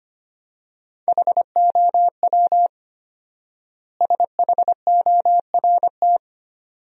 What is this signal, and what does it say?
Morse code, heard as a keyed tone, about 25 wpm, 700 Hz. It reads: HOW SHORT
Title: 4x Word-Spacing - Sentences from Top 400 Words - 25wpm